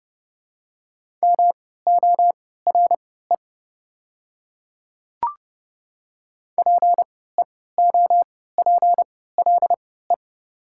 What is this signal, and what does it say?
Morse code, heard as a keyed tone, about 29 wpm, spaced as for 17 wpm, 700 Hz. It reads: MORE PEOPLE
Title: 3x Character Spacing - Top 200 Words - 30wpm